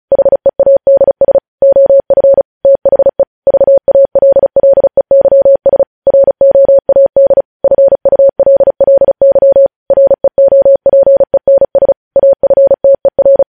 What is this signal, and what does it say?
Morse code, heard as a keyed tone, about 36 wpm, 550 Hz. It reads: HEADS OF THE VALLEYS ROAD FULLY REOPENS AFTER
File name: podcast / News headlines-1x-35x35